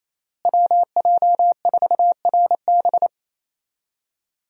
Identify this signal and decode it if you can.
Morse code, heard as a keyed tone, about 28 wpm, 700 Hz. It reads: WJ4RB
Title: USA Call Signs - 5 Characters - Repeated 3x - 28wpm